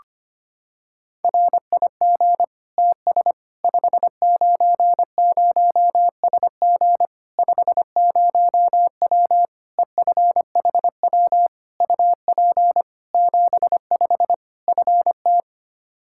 Morse code, heard as a keyed tone, about 25 wpm, 700 Hz. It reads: RIG TS 590SG 50W EFHW UP 75 FT
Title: Practice QSO - Exchange 2 - Info - 25wpm